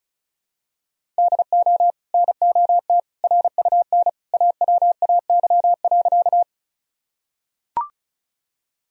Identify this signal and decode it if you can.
Morse code, heard as a keyed tone, about 34 wpm, 700 Hz. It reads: DO NOT RUN AWAY.
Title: Sentences from Top 1000 Words - Review Part 1 - 35wpm